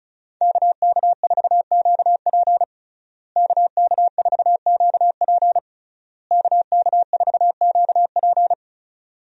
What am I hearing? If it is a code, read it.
Morse code, heard as a keyed tone, about 36 wpm, 700 Hz. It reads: KK4QP KK4QP KK4QP
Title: USA Call Signs - 5 Characters - Repeated 3x - 35wpm